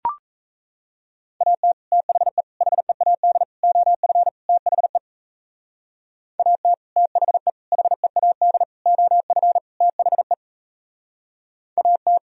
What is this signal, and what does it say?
Morse code, heard as a keyed tone, about 33 wpm, 700 Hz. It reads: AT THE HEAD OF THE AT THE HEAD OF THE AT
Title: Speed-Racing - Sets of 5 Words 28wpm